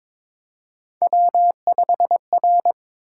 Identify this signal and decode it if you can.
Morse code, heard as a keyed tone, about 22 wpm, 700 Hz. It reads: W5R
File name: practice / 22wpm / Three Letter-Number Combinations - 22wpm